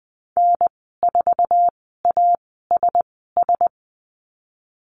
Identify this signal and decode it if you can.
Morse code, heard as a keyed tone, about 20 wpm, spaced as for 15 wpm, 700 Hz. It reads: N4ASS
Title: 048 Lesson 9g - Number 4 - Call Signs 20wpm